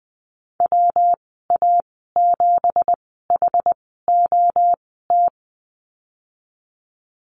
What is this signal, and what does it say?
Morse code, heard as a keyed tone, about 20 wpm, spaced as for 15 wpm, 700 Hz. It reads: WA7HOT
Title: 167 Lesson 27g - Number 7 - Call Signs 20wpm